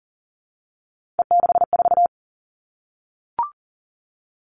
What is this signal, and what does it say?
Morse code, heard as a keyed tone, about 38 wpm, 690 Hz. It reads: E64
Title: Three Challenging Letter-Number Combinations 40wpm